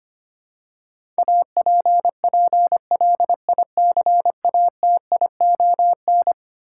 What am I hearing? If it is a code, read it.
Morse code, heard as a keyed tone, about 25 wpm, 700 Hz. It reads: APPLICATION
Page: morse-code-ninja-practice-25wpm